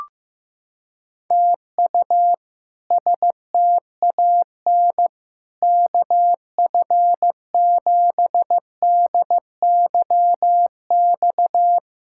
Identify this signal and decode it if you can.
Morse code, heard as a keyed tone, about 15 wpm, 700 Hz. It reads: TU STAN KF7DYX
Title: K1USN SST QSO Practice - 15wpm